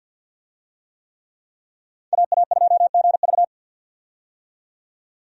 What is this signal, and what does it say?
Morse code, heard as a keyed tone, about 46 wpm, 700 Hz. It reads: AA2GV
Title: USA Call Signs - 5 Characters - 50wpm